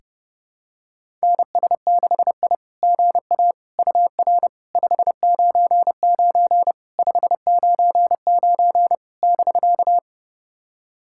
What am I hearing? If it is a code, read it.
Morse code, heard as a keyed tone, about 32 wpm, 700 Hz. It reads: NS6I GA UR 599 599 <BK>
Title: SOTA QSO Practice 30wpm